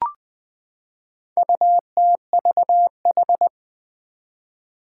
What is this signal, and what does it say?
Morse code, heard as a keyed tone, about 20 wpm, 700 Hz. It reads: UTVH
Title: Four Challenging Letter-Number Combinations 20wpm